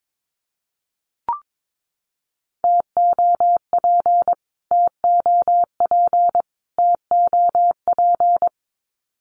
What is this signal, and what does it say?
Morse code, heard as a keyed tone, about 22 wpm, 700 Hz. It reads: TOP TOP TOP
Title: Top 500 Words Repeated 22wpm